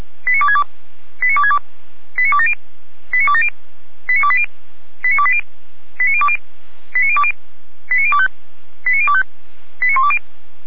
Five-tone selective call (ZVEI), heard as quick runs of five tones, tones of about 70 ms. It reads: 89262 89262 89270 89270 89280 89280 89020 89020 89025 89025 89120